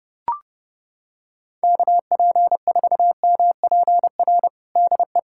Morse code, heard as a keyed tone, about 29 wpm, 700 Hz. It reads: KP4MPR DE
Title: USA Callsign DE Callsign - 6 Characters - 30wpm